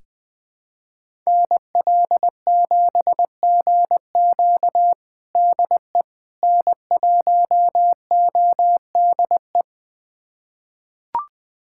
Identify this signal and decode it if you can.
Morse code, heard as a keyed tone, about 20 wpm, 700 Hz. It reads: NL7GQ DE N1ODE
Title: USA Callsign DE Callsign - 5 Characters - 20wpm